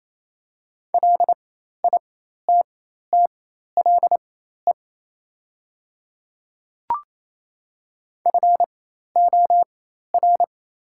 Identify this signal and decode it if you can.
Morse code, heard as a keyed tone, about 28 wpm, spaced as for 13 wpm, 700 Hz. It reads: LITTLE FOR
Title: 4x Character Spacing - Top 200 Words - 28wpm